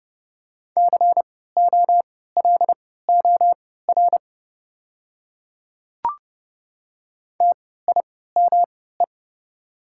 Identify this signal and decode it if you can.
Morse code, heard as a keyed tone, about 30 wpm, spaced as for 17 wpm, 690 Hz. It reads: COLOR TIME
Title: 3x Character Spacing - Top 500 Words - 30wpm